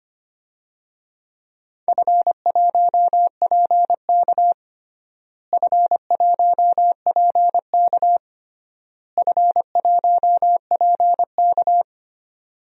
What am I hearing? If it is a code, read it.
Morse code, heard as a keyed tone, about 25 wpm, 700 Hz. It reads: F1PK F1PK F1PK
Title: DXCC-100 Call Signs - 4 Characters - Repeated 3x - 25wpm